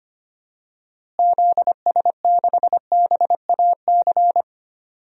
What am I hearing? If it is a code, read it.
Morse code, heard as a keyed tone, about 25 wpm, 700 Hz. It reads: ZS6BAC
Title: DXCC-100 Call Signs - 6 Characters - Repeated 3x - 25wpm